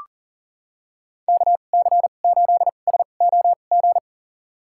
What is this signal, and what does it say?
Morse code, heard as a keyed tone, about 40 wpm, spaced as for 29 wpm, 700 Hz. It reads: KC8SOG
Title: USA Call Signs - 2x Character Spacing - 40wpm